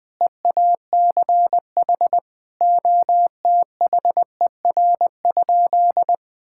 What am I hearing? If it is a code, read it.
Morse code, heard as a keyed tone, about 20 wpm, 700 Hz. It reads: EACH OTHER?
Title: Sentences from Top 1000 Words - Review Part 6 - 20wpm